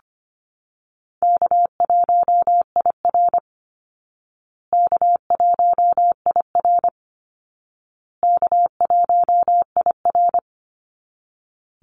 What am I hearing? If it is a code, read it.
Morse code, heard as a keyed tone, about 25 wpm, 700 Hz. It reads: K1IR K1IR K1IR
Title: Easy Call Signs Repeated 3x - 25wpm